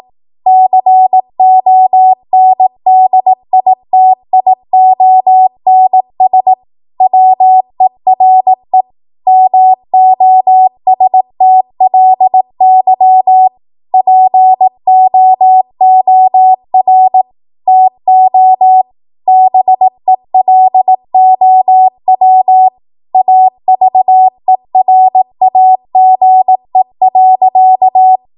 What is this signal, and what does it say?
Morse code, heard as a keyed tone, about 18 wpm, 750 Hz. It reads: CONDITIONS WERE MOSTLY POOR TO BELOW AVERAGE.
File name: Morse / 18WPM